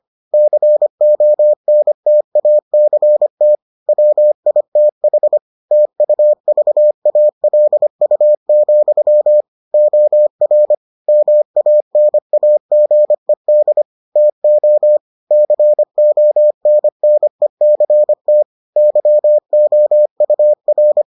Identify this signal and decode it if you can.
Morse code, heard as a keyed tone, about 25 wpm, 600 Hz. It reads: CONTACT WITH TUVALU, OR MANAGED TO CONNECT YOUR